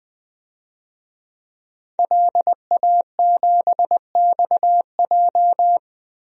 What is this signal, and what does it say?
Morse code, heard as a keyed tone, about 20 wpm, 700 Hz. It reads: LA7XJ
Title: DXCC-100 Call Signs - 5 Characters - 20wpm